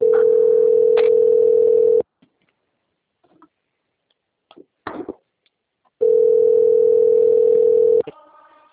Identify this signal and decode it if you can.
Ringback tone (North America), 440 + 480 Hz.
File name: out- - - - - .42000